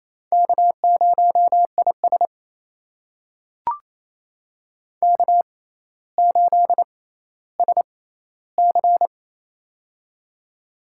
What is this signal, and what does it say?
Morse code, heard as a keyed tone, about 28 wpm, spaced as for 14 wpm, 700 Hz. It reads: K0IS K8SC
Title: USA Call Signs - 6x Character Spacing - 28wpm